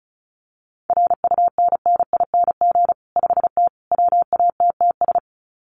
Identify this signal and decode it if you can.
Morse code, heard as a keyed tone, about 34 wpm, 700 Hz. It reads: RUNNING 5T WATTS
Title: Common QSO Phrases 35wpm